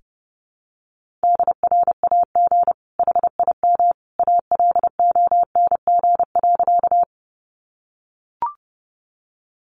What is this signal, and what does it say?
Morse code, heard as a keyed tone, about 30 wpm, 700 Hz. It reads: DRAG HIM ALONG.
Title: Sentences from Top 1500 Words 30wpm